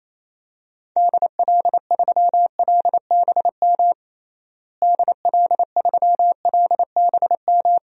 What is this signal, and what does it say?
Morse code, heard as a keyed tone, about 28 wpm, 700 Hz. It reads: DL3LBM DL3LBM
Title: DXCC-100 Call Signs - 6 Characters - Repeated 3x - 28wpm